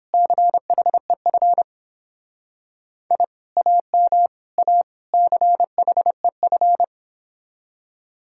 Morse code, heard as a keyed tone, about 31 wpm, 700 Hz. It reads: CHEF I AM A CHEF